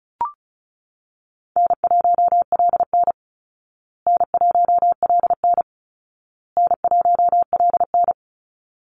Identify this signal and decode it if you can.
Morse code, heard as a keyed tone, about 34 wpm, 700 Hz. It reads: N1LN N1LN N1LN
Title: Call Signs Repeated 2x - 35wpm